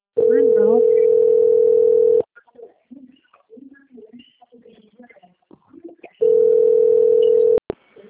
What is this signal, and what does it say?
Ringback tone (North America), 440 + 480 Hz, with speech.
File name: out- - - - - .11223